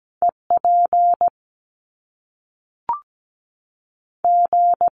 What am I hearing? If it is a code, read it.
Morse code, heard as a keyed tone, about 17 wpm, spaced as for 3 wpm, 700 Hz. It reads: E PG
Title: Top Two Letter Combinations in Words 17wpm